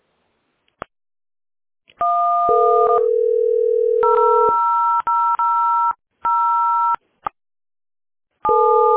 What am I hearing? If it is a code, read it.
DTMF, heard as telephone touch tones, tones of about 920 ms. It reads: 1 000